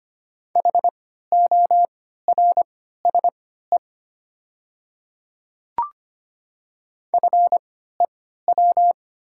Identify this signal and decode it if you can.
Morse code, heard as a keyed tone, about 25 wpm, spaced as for 14 wpm, 700 Hz. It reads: HORSE FEW
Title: 3x Character Spacing - Top 500 Words - 25wpm